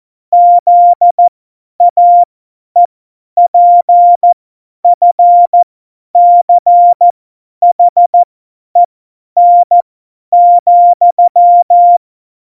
Morse code, heard as a keyed tone, about 14 wpm, spaced as for 10 wpm, 700 Hz. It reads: ZAEPFCHEN,
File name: Morsekurs / Witze / Witz 13. Oktober 2020, 50 F70